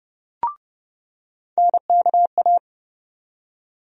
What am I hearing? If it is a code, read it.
Morse code, heard as a keyed tone, about 32 wpm, 700 Hz. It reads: NKA
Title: USA Call Sign Postfixes - 3 Letters 30wpm